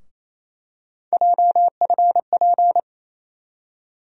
Morse code, heard as a keyed tone, about 28 wpm, 700 Hz. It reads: JFP
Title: USA Call Sign Postfixes - 3 Letters 28wpm